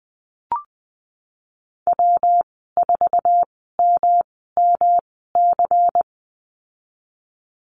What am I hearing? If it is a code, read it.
Morse code, heard as a keyed tone, about 20 wpm, spaced as for 15 wpm, 700 Hz. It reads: W4MMC
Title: 147 Lesson 24g - Letter y - Call Signs 20wpm